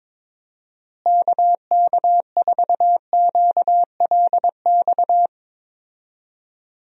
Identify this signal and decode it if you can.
Morse code, heard as a keyed tone, about 22 wpm, 700 Hz. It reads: KK4QLX
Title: USA Call Signs - 6 Characters - Repeated 3x - 22wpm